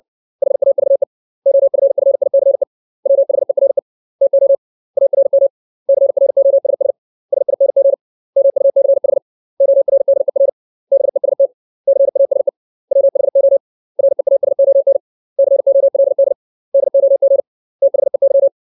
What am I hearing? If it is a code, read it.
Morse code, heard as a keyed tone, about 60 wpm, 550 Hz. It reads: HAVE OFFICE WHERE TO ARM CROSS SING MUCH ORDER BEST CASE WHO REASON COLD DOG THEY
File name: 632 words / 632 Words 60WPM